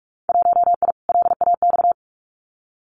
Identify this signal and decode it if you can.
Morse code, heard as a keyed tone, about 44 wpm, 700 Hz. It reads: 1I LAX
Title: Winter Field Day Exchanges Encore 45wpm